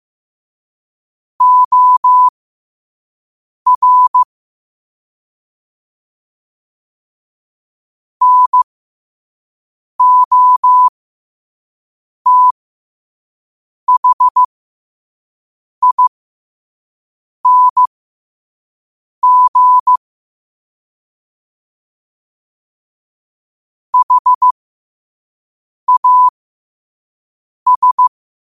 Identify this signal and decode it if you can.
Morse code, heard as a keyed tone, about 15 wpm, spaced as for 5 wpm, 1000 Hz. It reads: OR NOTHING HAS
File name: qotdfiles / 5wpm